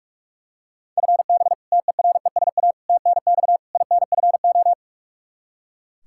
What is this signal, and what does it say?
Morse code, heard as a keyed tone, about 41 wpm, 700 Hz. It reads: FB TERESA TNX INFO
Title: Practice QSO - Exchange 3 - More Info - 45wpm